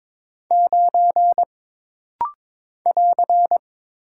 Morse code, heard as a keyed tone, about 22 wpm, 700 Hz. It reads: 9 <AR>
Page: This rapid-fire timing is geared to push your ICR (Instant Character Recognition) skills.